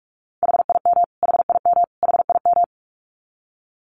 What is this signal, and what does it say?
Morse code, heard as a keyed tone, about 44 wpm, 700 Hz. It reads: HIM HIM HIM
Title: Top 200 Words Repeated 45wpm